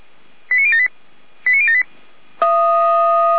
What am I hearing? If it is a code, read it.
Five-tone selective call (ZVEI), heard as quick runs of five tones, tones of about 70 ms. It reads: 89078 89078
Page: SDR/ZVEI